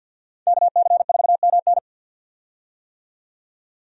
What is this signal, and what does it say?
Morse code, heard as a keyed tone, about 49 wpm, 700 Hz. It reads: KC4MN
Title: USA Call Signs - 5 Characters - 50wpm